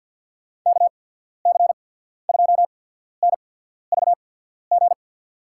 Morse code, heard as a keyed tone, about 48 wpm, spaced as for 14 wpm, 700 Hz. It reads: KC2NVG
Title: USA Call Signs - 6 Characters - 8x Character Spacing - 50wpm